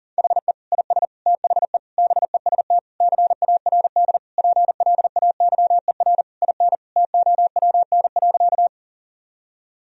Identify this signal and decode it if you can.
Morse code, heard as a keyed tone, about 40 wpm, 700 Hz. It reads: HE IS THE BEST CARD PLAYER IN TOWN.
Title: Sentences from Top 1000 Words - Review Part 4 - 40wpm